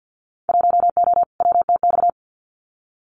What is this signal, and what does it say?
Morse code, heard as a keyed tone, about 50 wpm, 700 Hz. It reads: 1O WTX
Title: Winter Field Day Exchanges Encore 50wpm